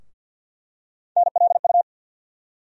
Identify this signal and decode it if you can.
Morse code, heard as a keyed tone, about 47 wpm, 700 Hz. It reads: NLU
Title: USA Call Sign Postfixes - 3 Letters 50wpm